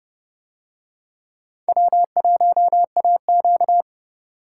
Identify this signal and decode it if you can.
Morse code, heard as a keyed tone, about 29 wpm, 700 Hz. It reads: W1AQ